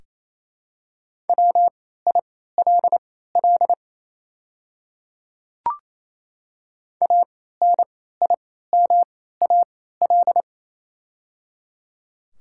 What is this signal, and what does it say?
Morse code, heard as a keyed tone, about 28 wpm, spaced as for 16 wpm, 700 Hz. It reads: WILL ANIMAL
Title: 3x Character Spacing - Top 200 Words - 28wpm